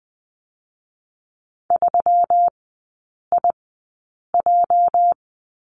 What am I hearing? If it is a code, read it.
Morse code, heard as a keyed tone, about 20 wpm, spaced as for 8 wpm, 700 Hz. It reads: 3IJ
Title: 205 Lesson 33e - Letter j - Three Characters 20wpm